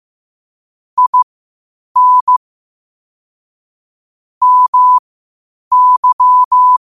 Morse code, heard as a keyed tone, about 15 wpm, spaced as for 8 wpm, 1000 Hz. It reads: IN MY
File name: qotdfiles / 8wpm